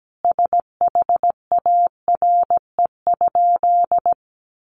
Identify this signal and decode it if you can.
Morse code, heard as a keyed tone, about 17 wpm, 700 Hz. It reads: SHARE?
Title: Sentences from Top 900 Words 17wpm